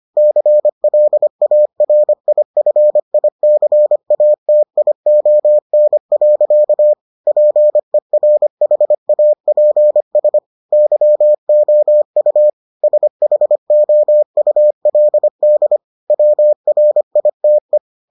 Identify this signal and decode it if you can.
Morse code, heard as a keyed tone, about 25 wpm, 600 Hz. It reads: CLARIFICATION. PERHAPS YOU SHOULD WRITE